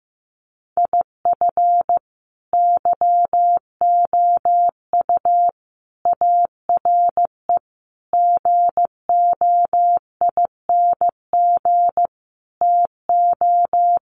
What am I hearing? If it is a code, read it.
Morse code, heard as a keyed tone, about 15 wpm, 700 Hz. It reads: IF YOU ARE GOING TO
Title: Sets of 5 Words Encore2 15wpm